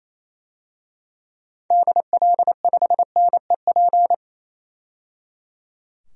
Morse code, heard as a keyed tone, about 28 wpm, 700 Hz. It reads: DL5NEP
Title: DXCC-100 Call Signs - 6 Characters - 28wpm